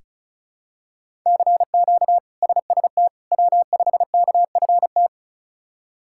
Morse code, heard as a keyed tone, about 36 wpm, 700 Hz. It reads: CQ SST W5KFT
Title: K1USN SST QSO Practice - 35wpm